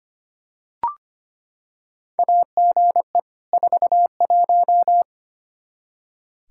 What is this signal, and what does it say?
Morse code, heard as a keyed tone, about 25 wpm, 700 Hz. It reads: AGE 41
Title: Practice QSO - Exchange 3 - More Info - 25wpm